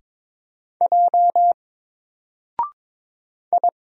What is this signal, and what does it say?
Morse code, heard as a keyed tone, about 22 wpm, 700 Hz. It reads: J I
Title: Single Letter-Number 22wpm